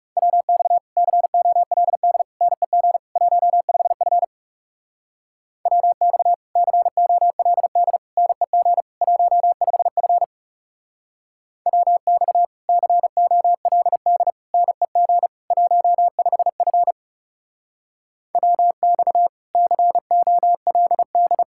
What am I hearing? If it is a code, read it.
Morse code, heard as a keyed tone, about 37 wpm, 700 Hz. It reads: WX COLD NEG 15F WX COLD NEG 15F WX COLD NEG 15F WX COLD
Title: Speed-Racing Common QSO Phrases - 30wpm